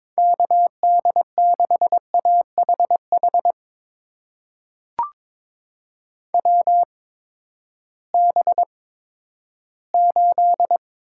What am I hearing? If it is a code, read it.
Morse code, heard as a keyed tone, about 22 wpm, spaced as for 13 wpm, 700 Hz. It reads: KD6AHH WB8
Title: USA Call Signs - 6 Characters - 8x Character Spacing - 22wpm